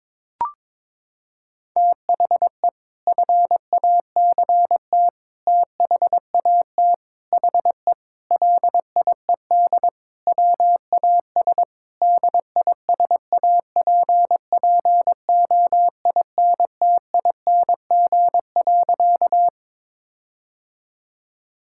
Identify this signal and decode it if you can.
Morse code, heard as a keyed tone, about 22 wpm, 700 Hz. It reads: THE FACT THAT HE LIED WAS DISAPPOINTING.
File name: practice / 22wpm / Sentences with 4-Word Phrases - 22wpm